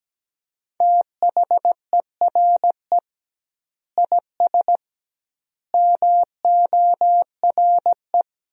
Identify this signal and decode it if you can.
Morse code, heard as a keyed tone, about 17 wpm, 700 Hz. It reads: THERE IS MORE
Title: 2x Word-Spacing - Sentences from Top 200 Words - 17wpm